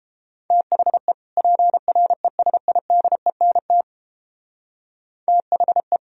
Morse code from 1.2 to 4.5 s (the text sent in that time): PRESIDENT